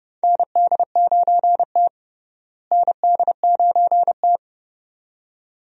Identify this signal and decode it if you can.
Morse code, heard as a keyed tone, about 30 wpm, 700 Hz. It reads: ND9T ND9T
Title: USA Call Signs - 4 Characters - Repeated 2x - 30wpm